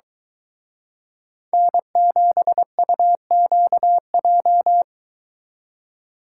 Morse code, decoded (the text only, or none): N7UQJ